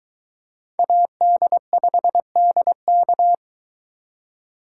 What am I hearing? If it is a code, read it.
Morse code, heard as a keyed tone, about 24 wpm, 700 Hz. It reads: AD5DK